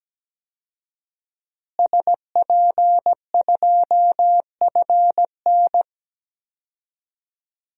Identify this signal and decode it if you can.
Morse code, heard as a keyed tone, about 17 wpm, 700 Hz. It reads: SP2FN
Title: DXCC-100 Call Signs - 5 Characters - Repeated 2x - 17wpm